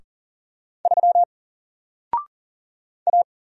Morse code, heard as a keyed tone, about 40 wpm, 700 Hz. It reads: 3 A
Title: Single Letter-Number Alt 40wpm